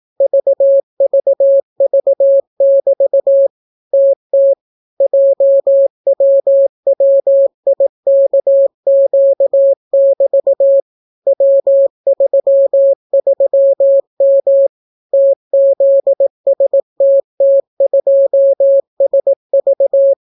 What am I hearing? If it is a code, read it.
Morse code, heard as a keyed tone, about 18 wpm, 550 Hz. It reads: VVV<BT> TT JWWIKQ<BT> W33M TZSTT2SV